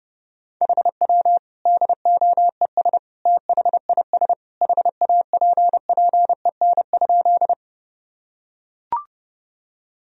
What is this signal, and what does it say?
Morse code, heard as a keyed tone, about 30 wpm, 700 Hz. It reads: HW DOES THIS HAPPEN?
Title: Sentences from Top 400 Words and Abbreviations 30wpm